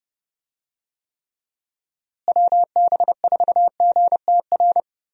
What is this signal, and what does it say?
Morse code, heard as a keyed tone, about 30 wpm, 700 Hz. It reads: WB4GTR
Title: USA Call Signs - 6 Characters - Repeated 3x - 30wpm